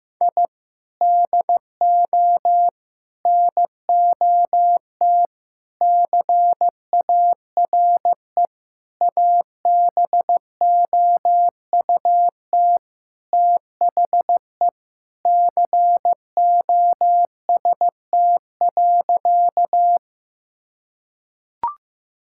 Morse code, heard as a keyed tone, about 15 wpm, 700 Hz. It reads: I DO NOT CARE ABOUT THE COST.
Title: Sentences from Top 1000 Words - Review Part 5 - 15wpm